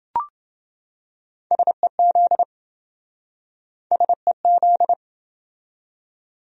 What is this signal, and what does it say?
Morse code, heard as a keyed tone, about 27 wpm, 700 Hz. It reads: SEZ SEZ